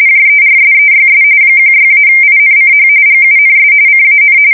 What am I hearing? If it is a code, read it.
RTTY, about 45.45 baud. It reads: JERSEYS 16-INCH GUNS CAN
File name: fsk170 / battleshipnj